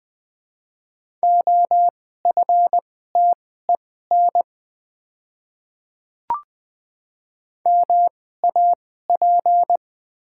Morse code, heard as a keyed tone, about 20 wpm, spaced as for 14 wpm, 700 Hz. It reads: OFTEN MAP
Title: 2x Character Spacing - Top 500 Words - 20wpm